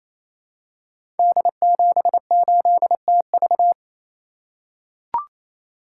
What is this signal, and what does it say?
Morse code, heard as a keyed tone, about 27 wpm, 700 Hz. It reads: D78TV
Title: Code Groups - 5 Letter:Numbers - 28wpm